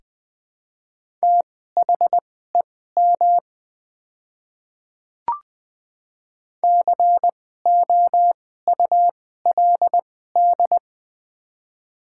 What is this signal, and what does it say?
Morse code, heard as a keyed tone, about 20 wpm, spaced as for 14 wpm, 700 Hz. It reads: THEM COULD
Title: 2x Character Spacing - Top 300 Words - 20wpm